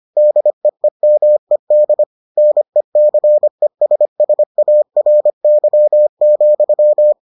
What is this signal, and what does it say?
Morse code, heard as a keyed tone, about 25 wpm, 600 Hz. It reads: DEEMED NECESSARY,